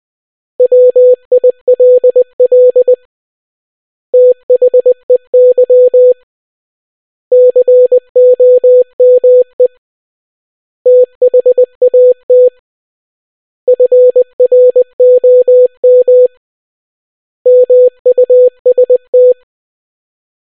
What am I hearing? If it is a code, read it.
Morse code, heard as a keyed tone, about 20 wpm, 500 Hz. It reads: WILL THEY COME THAT FROM MUST